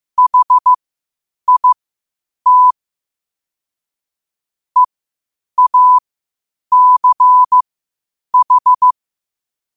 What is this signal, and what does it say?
Morse code, heard as a keyed tone, about 15 wpm, spaced as for 8 wpm, 1000 Hz. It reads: HIT EACH
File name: qotdfiles / 8wpm